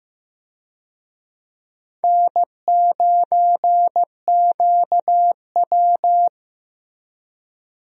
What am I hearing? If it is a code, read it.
Morse code, heard as a keyed tone, about 15 wpm, 700 Hz. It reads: N9QW
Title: USA Call Signs - 4 Characters - Repeated 2x - 15wpm